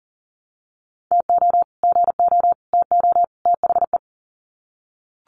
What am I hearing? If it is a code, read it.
Morse code, heard as a keyed tone, about 40 wpm, 700 Hz. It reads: TO GO TO THE
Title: Sets of 4 Words 40wpm